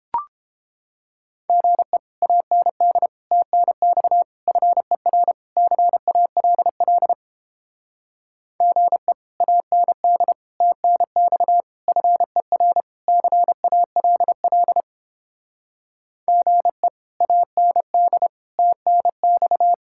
Morse code, heard as a keyed tone, about 29 wpm, 700 Hz. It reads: GE AND TNX FER CALL GE AND TNX FER CALL GE AND TNX